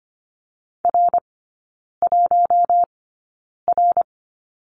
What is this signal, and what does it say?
Morse code, heard as a keyed tone, about 25 wpm, spaced as for 9 wpm, 700 Hz. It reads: R1R
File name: course / 25:10wpm / 093 Lesson 16e - Letter c - Three Characters 25wpm